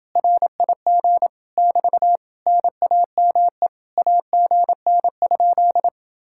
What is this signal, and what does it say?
Morse code, heard as a keyed tone, about 29 wpm, 700 Hz. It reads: RIG <BT> NAME AGN?